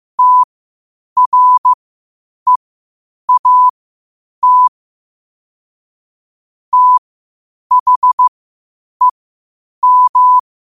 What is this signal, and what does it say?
Morse code, heard as a keyed tone, about 15 wpm, spaced as for 8 wpm, 1000 Hz. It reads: TREAT THEM
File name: qotdfiles / 8wpm